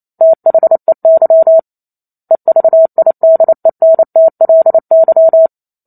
Morse code, heard as a keyed tone, about 28 wpm, 650 Hz. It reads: THEY EVIDENTLY